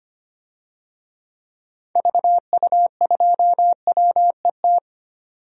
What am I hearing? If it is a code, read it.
Morse code, heard as a keyed tone, about 25 wpm, 700 Hz. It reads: VU2WET
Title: DXCC-100 Call Signs - 6 Characters - 25wpm